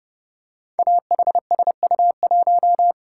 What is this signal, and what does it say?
Morse code, heard as a keyed tone, about 29 wpm, 700 Hz. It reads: AHSU1
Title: Code Groups - 5 Letter:Numbers - 30wpm